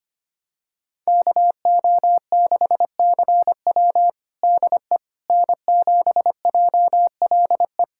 Morse code, heard as a keyed tone, about 25 wpm, 700 Hz. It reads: KO6CW DE N7JLE
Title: USA Callsign DE Callsign - 5 Characters - 25wpm